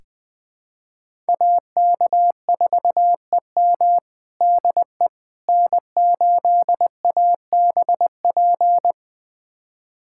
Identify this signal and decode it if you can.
Morse code, heard as a keyed tone, about 20 wpm, 700 Hz. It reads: AK4EM DE N8ABP
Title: USA Callsign DE Callsign - 5 Characters - 20wpm